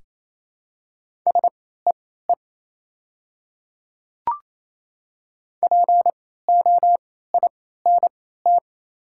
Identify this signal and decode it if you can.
Morse code, heard as a keyed tone, about 28 wpm, spaced as for 16 wpm, 700 Hz. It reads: SEE POINT
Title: 3x Character Spacing - Top 500 Words - 28wpm